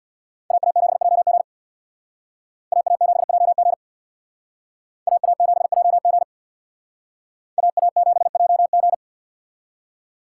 Morse code, heard as a keyed tone, about 62 wpm, 690 Hz. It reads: AA7JG AA7JG AA7JG AA7JG
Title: Speed Racing - USA 2x2 Call Signs 50wpm